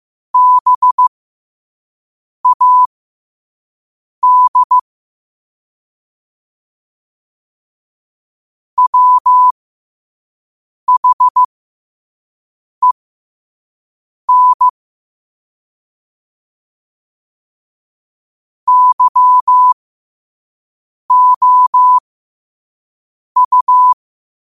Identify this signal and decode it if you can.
Morse code, heard as a keyed tone, about 15 wpm, spaced as for 5 wpm, 1000 Hz. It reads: BAD WHEN YOU
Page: qotd-at-5-wpm